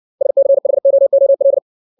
Morse code, heard as a keyed tone, about 60 wpm, 550 Hz. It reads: SCHOOL
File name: 632 words / 632 Words 60WPM